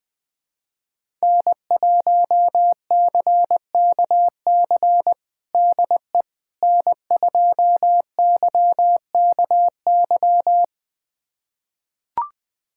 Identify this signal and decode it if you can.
Morse code, heard as a keyed tone, about 20 wpm, 700 Hz. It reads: N1CKC DE N2YKY
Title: USA Callsign DE Callsign - 5 Characters - 20wpm